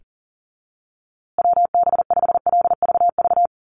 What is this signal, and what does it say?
Morse code, heard as a keyed tone, about 37 wpm, 700 Hz. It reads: WB5LVV
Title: USA Call Signs Repeated 2x 40wpm